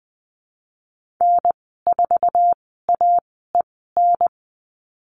Morse code, heard as a keyed tone, about 20 wpm, spaced as for 14 wpm, 700 Hz. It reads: N4AEN
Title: 082 Lesson 14g - Number 2 - Call Signs 20wpm